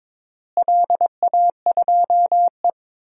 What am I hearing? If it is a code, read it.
Morse code, heard as a keyed tone, about 22 wpm, 700 Hz. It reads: LA2E
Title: DXCC-100 Call Signs - 4 Characters - Repeated 3x - 22wpm